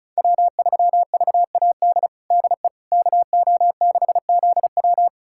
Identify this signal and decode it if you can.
Morse code, heard as a keyed tone, about 36 wpm, 700 Hz. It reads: W3VAD DE KO6ZW
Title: USA Callsign DE Callsign - 5 Characters - 35wpm